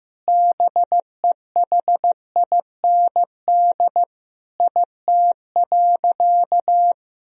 Morse code, heard as a keyed tone, about 16 wpm, 700 Hz. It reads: BEHIND IT.
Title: Sentences from Top 500 Words and Abbreviations 15wpm